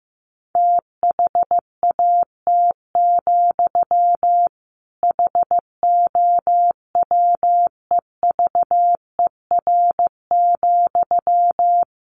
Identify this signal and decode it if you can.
Morse code, heard as a keyed tone, about 14 wpm, 700 Hz. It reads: THAT, HOWEVER,
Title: Sentences from Top 600 Words 15wpm